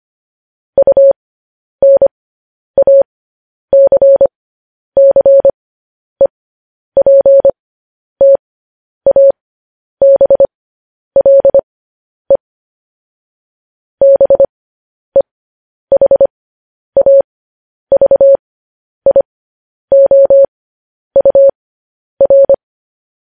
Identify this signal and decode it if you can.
Morse code, heard as a keyed tone, about 25 wpm, spaced as for 10 wpm, 550 Hz. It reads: UNACCEPTABLE BEHAVIOUR
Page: News Headlines in Morse Code at 10 WPM